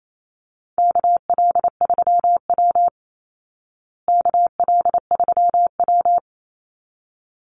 Morse code, heard as a keyed tone, about 28 wpm, 700 Hz. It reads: KL3W KL3W
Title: Call Signs Repeated 3x - 28wpm